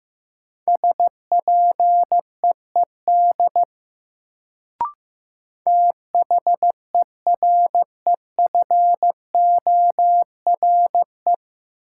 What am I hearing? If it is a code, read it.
Morse code, heard as a keyed tone, about 15 wpm, 700 Hz. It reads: SPEED THEREFORE
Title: Sentences from Top 800 Words - New Word Drill - 15wpm